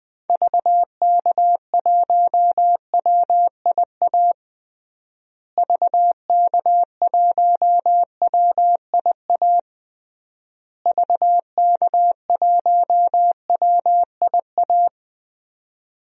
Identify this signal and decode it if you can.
Morse code, heard as a keyed tone, about 21 wpm, 700 Hz. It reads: VK1WIA VK1WIA VK1WIA
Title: DXCC-100 Call Signs - 6 Characters - Repeated 3x - 20wpm